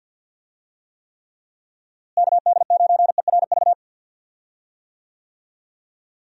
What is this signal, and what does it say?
Morse code, heard as a keyed tone, about 47 wpm, 700 Hz. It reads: KD9ERV
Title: USA Call Signs - 6 Characters - Repeated 3x - 50wpm